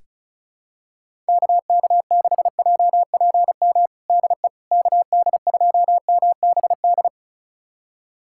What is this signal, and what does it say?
Morse code, heard as a keyed tone, about 36 wpm, 700 Hz. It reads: KK6JPM DE KD2MBD
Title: USA Callsign DE Callsign - 6 Characters - 35wpm